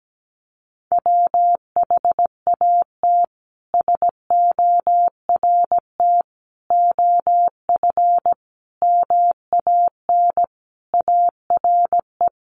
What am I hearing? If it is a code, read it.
Morse code, heard as a keyed tone, about 17 wpm, 700 Hz. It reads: WHAT SORT OF MAN ARE
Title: Sentences from Top 1300 Words 17wpm